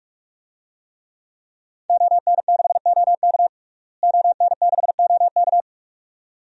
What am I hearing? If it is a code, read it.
Morse code, heard as a keyed tone, about 42 wpm, 700 Hz. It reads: ON6OK ON6OK
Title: DXCC-100 Call Signs - 5 Characters - Repeated 2x - 45wpm